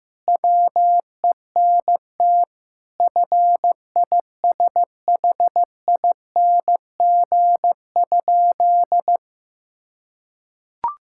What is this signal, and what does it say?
Morse code, heard as a keyed tone, about 15 wpm, 700 Hz. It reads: WENT FISHING?
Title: Sentences from Top 1000 Words - Review Part 5 - 15wpm